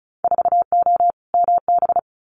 Morse code, heard as a keyed tone, about 34 wpm, 700 Hz. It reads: 4O MB
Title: Winter Field Day Exchanges Encore 35wpm